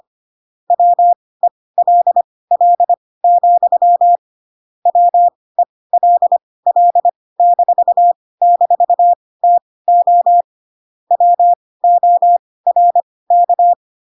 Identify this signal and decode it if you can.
Morse code, heard as a keyed tone, about 25 wpm, spaced as for 18 wpm, 700 Hz. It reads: WELL, WELL--TO WORK
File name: Pinocchio / Pinocchio ch1 12 25 ik1hge